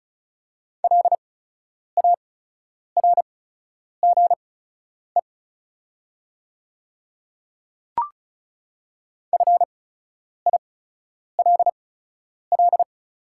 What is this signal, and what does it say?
Morse code, heard as a keyed tone, about 36 wpm, spaced as for 10 wpm, 690 Hz. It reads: LARGE FILL
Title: 8x Character Spacing - Top 500 Words - 35wpm